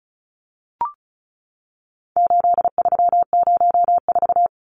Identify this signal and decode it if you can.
Morse code, heard as a keyed tone, about 34 wpm, 700 Hz. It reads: 8304
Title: Numbers - 4-Digit 35wpm